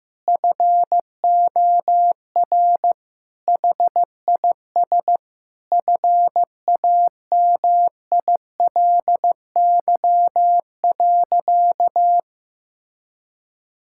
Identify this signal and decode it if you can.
Morse code, heard as a keyed tone, about 16 wpm, 700 Hz. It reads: FOR HIS FAMILY.
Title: Sentences from Top 1000 Words - Review Part 5 - 15wpm